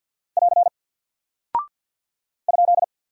Morse code, heard as a keyed tone, about 48 wpm, 700 Hz. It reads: <AR> ?
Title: Single Character (Letter:Number:Punctuation:Prosign) Alt 50wpm